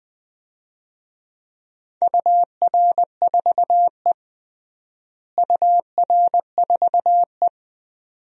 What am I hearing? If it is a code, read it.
Morse code, heard as a keyed tone, about 20 wpm, 700 Hz. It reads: UR4E UR4E
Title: DXCC-100 Call Signs - 4 Characters - Repeated 2x - 20wpm